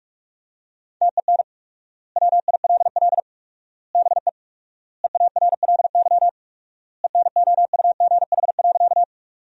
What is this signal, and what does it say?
Morse code, heard as a keyed tone, about 46 wpm, 700 Hz. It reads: TEN WILL BE EARLY ENOUGH.
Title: 4x Word-Spacing - Sentences from Top 400 Words - 45wpm